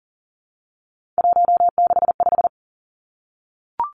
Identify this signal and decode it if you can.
Morse code, heard as a keyed tone, about 38 wpm, 700 Hz. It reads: 165
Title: Three Challenging Letter-Number Combinations 40wpm